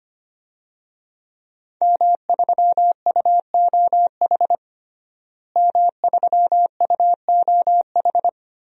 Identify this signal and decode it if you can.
Morse code, heard as a keyed tone, about 25 wpm, 700 Hz. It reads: M3UOH M3UOH
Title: DXCC-100 Call Signs - 5 Characters - Repeated 2x - 25wpm